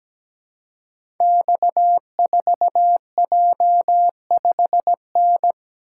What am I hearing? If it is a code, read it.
Morse code, heard as a keyed tone, about 17 wpm, 700 Hz. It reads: X4J5N
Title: Code Groups - 5 Letter:Numbers - 17wpm